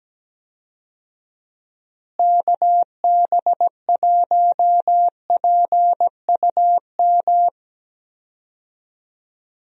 Morse code, heard as a keyed tone, about 17 wpm, 700 Hz. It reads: KB1PUM
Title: USA Call Signs - 6 Characters - Repeated 2x - 17wpm